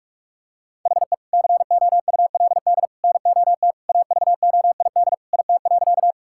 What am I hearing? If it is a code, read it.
Morse code, heard as a keyed tone, about 41 wpm, 700 Hz. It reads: HE COULD NOT AVOID IT.
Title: Sentences from Top 1000 Words - Review Part 6 - 45wpm